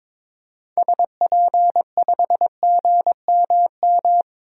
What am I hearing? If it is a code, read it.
Morse code, heard as a keyed tone, about 22 wpm, 700 Hz. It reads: SP5GMM
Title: DXCC-100 Call Signs - 6 Characters - Repeated 2x - 22wpm